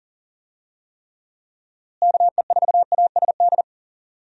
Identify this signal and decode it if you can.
Morse code, heard as a keyed tone, about 40 wpm, 700 Hz. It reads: KE4ASD
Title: USA Call Signs - 6 Characters - Repeated 2x - 40wpm